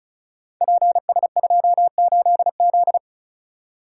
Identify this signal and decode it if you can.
Morse code, heard as a keyed tone, about 36 wpm, 690 Hz. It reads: PS28Z
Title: Code Groups - 5 Letter:Numbers - 35wpm